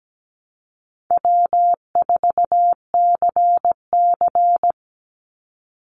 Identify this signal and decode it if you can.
Morse code, heard as a keyed tone, about 17 wpm, 700 Hz. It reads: W4CC
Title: Easy Call Signs - 17wpm